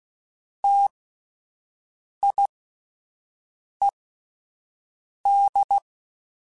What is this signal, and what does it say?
Morse code, heard as a keyed tone, about 16 wpm, spaced as for 5 wpm, 800 Hz. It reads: TIED
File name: T Prefixes / T mp3s